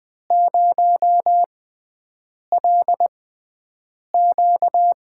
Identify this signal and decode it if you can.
Morse code, heard as a keyed tone, about 20 wpm, spaced as for 7 wpm, 700 Hz. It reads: 0LQ
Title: USA Call Signs - 6x Character Spacing - 20wpm